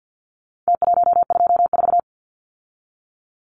Morse code, heard as a keyed tone, about 50 wpm, 700 Hz. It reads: T124
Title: Numbers - Contest Style - 4-Digit 50wpm